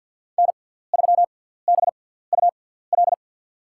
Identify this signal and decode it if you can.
Morse code, heard as a keyed tone, about 51 wpm, spaced as for 17 wpm, 690 Hz. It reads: N3BUL
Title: USA Call Signs - 5 Characters - 6x Character Spacing - 50wpm